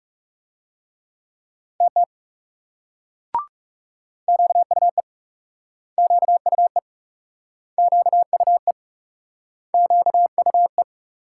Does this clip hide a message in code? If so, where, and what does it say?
Yes: Morse code from 1.8 to 10.8 s, heard as a keyed tone, about 42 wpm, 700 Hz. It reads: TT QUE QUE QUE QUE